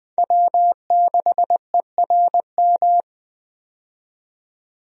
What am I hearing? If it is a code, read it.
Morse code, heard as a keyed tone, about 21 wpm, 700 Hz. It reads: W6ERM
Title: USA Call Signs - 5 Characters - 20wpm